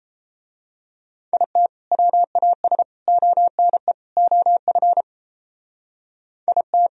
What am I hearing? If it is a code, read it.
Morse code, heard as a keyed tone, about 30 wpm, 700 Hz. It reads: IT WAS ONE OF IT
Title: Speed-Racing - Sets of 4 Words 28wpm